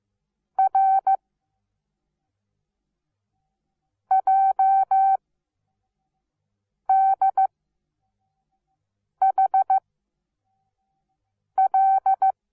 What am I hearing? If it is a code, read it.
Morse code, heard as a keyed tone, about 15 wpm, 770 Hz. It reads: R J D H L
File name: Onlinelehrgang / morsen / DJ4UF Morsen17